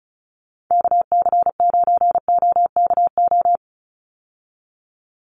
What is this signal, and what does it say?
Morse code, heard as a keyed tone, about 34 wpm, 700 Hz. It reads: KC9OKO
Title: USA Call Signs 35wpm